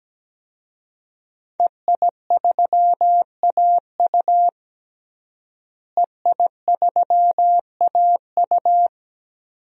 Morse code, heard as a keyed tone, about 17 wpm, 700 Hz. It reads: EI3AU EI3AU
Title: DXCC-100 Call Signs - 5 Characters - Repeated 2x - 17wpm